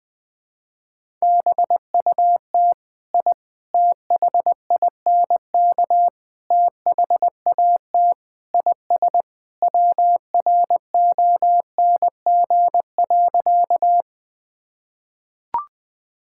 Morse code, heard as a keyed tone, about 19 wpm, 700 Hz. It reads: BUT I THINK THAT IS WRONG.
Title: Sentences with 4-Word Phrases - 20wpm